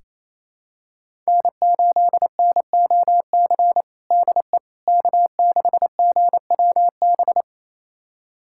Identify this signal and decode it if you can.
Morse code, heard as a keyed tone, about 28 wpm, 700 Hz. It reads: N8NOC DE K6GWB
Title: USA Callsign DE Callsign - 5 Characters - 28wpm